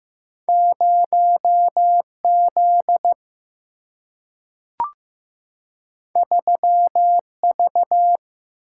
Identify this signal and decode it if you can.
Morse code, heard as a keyed tone, about 15 wpm, 700 Hz. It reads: 0 Z 3 V
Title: Single-Letter Single-Number 15wpm